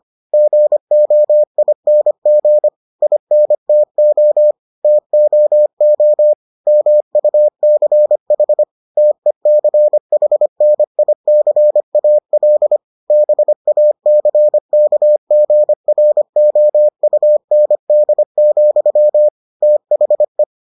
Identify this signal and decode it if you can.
Morse code, heard as a keyed tone, about 25 wpm, 600 Hz. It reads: GOING INTO TOO MUCH TECHNICAL BACKGROUND, THE